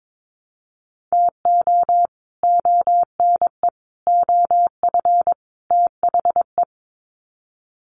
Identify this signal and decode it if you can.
Morse code, heard as a keyed tone, about 22 wpm, 700 Hz. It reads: TO ONE OF THE
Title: Sets of 4 Words 22wpm